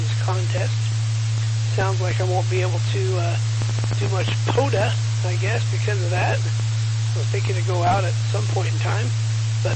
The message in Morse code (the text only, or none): none